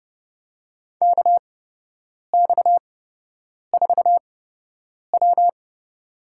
Morse code, heard as a keyed tone, about 31 wpm, spaced as for 8 wpm, 700 Hz. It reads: KX4W
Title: USA Call Signs - 8x Character Spacing - 30wpm